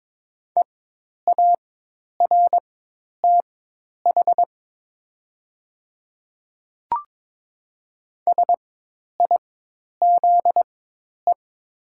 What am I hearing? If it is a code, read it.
Morse code, heard as a keyed tone, about 22 wpm, spaced as for 10 wpm, 700 Hz. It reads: EARTH SIZE
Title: 4x Character Spacing - Top 500 Words - 22wpm